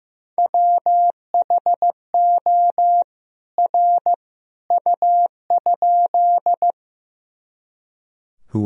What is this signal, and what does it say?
Morse code, heard as a keyed tone, about 15 wpm, 700 Hz. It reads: WHO R U?
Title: Sentences from Top 100 Words and Abbreviations 15wpm